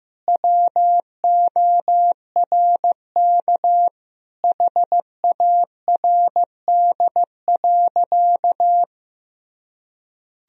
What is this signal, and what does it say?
Morse code, heard as a keyed tone, about 16 wpm, 700 Hz. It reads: WORK HARD.
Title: Sentences from Top 1000 Words - Review Part 1 - 15wpm